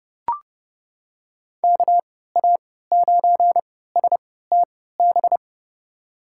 Morse code, heard as a keyed tone, about 30 wpm, spaced as for 17 wpm, 700 Hz. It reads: KA9STB
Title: USA Call Signs - 3x Character Spacing - 30wpm